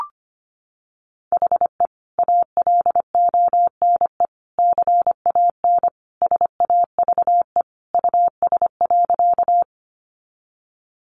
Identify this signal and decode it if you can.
Morse code, heard as a keyed tone, about 25 wpm, 690 Hz. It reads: HE ALONE CAN SAVE US.
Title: Sentences from Top 1100 Words 25wpm